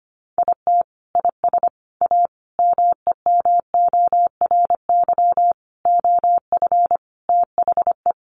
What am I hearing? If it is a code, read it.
Morse code, heard as a keyed tone, about 25 wpm, 700 Hz. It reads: IT IS A MEMORY OF THE
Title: Sentences from Top 1300 Words 25wpm